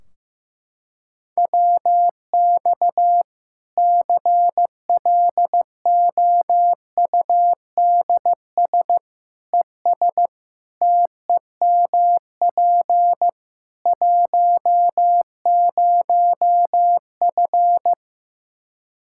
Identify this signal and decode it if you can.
Morse code, heard as a keyed tone, about 16 wpm, 700 Hz. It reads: WX CLOUDS ES TEMP 10F
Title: Practice QSO - Exchange 2 - Info - 15wpm